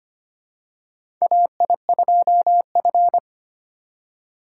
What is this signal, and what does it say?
Morse code, heard as a keyed tone, about 25 wpm, 700 Hz. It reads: AI2F
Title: USA Call Signs - 4 Characters - 2x Character Spacing - 25wpm